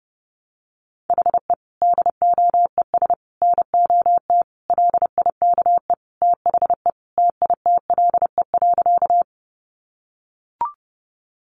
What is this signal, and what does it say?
Morse code, heard as a keyed tone, about 30 wpm, 700 Hz. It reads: HE DOES NOT LIKE THE TITLE.
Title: Sentences from Top 1200 Words 30wpm